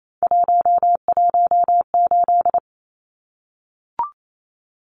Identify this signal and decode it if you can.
Morse code, heard as a keyed tone, about 28 wpm, 700 Hz. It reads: 118
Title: Numbers - 3-Digit 28wpm